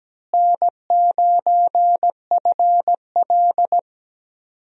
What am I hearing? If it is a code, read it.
Morse code, heard as a keyed tone, about 17 wpm, 700 Hz. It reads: N9FL
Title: USA Call Signs - 4 Characters - Repeated 3x - 17wpm